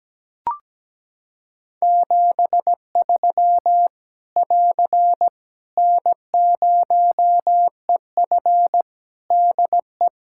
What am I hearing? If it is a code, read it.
Morse code, heard as a keyed tone, about 17 wpm, 700 Hz. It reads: 73 <AR> N0EF DE
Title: Practice QSO - Exchange 4 - Ending - 17wpm